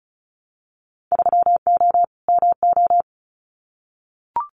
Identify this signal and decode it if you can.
Morse code, heard as a keyed tone, about 34 wpm, 700 Hz. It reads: 3O MO
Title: Winter Field Day Exchanges Encore 35wpm